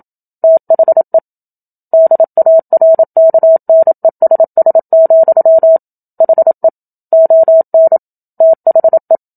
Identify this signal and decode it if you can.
Morse code, heard as a keyed tone, about 27 wpm, 650 Hz. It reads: THE DARKNESS, HE ON THE
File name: Book Two / 01-Chapter 1 - 27 WPM